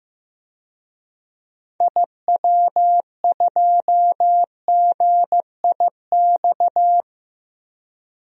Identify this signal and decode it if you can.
Morse code, heard as a keyed tone, about 15 wpm, 700 Hz. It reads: IW2GIX
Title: DXCC-100 Call Signs - 6 Characters - Repeated 2x - 15wpm